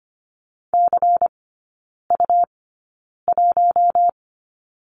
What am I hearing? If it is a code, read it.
Morse code, heard as a keyed tone, about 25 wpm, spaced as for 9 wpm, 700 Hz. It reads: CU1
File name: Lesson 17e - Letter u - Three Characters 25wpm